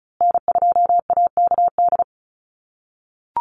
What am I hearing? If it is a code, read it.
Morse code, heard as a keyed tone, about 34 wpm, 700 Hz. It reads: N2AKD
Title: USA Call Signs Repeated 3x 35wpm